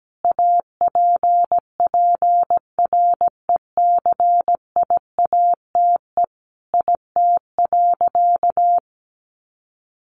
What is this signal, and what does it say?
Morse code, heard as a keyed tone, about 17 wpm, 700 Hz. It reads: APPRECIATE IT.
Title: Sentences from Top 1300 Words 17wpm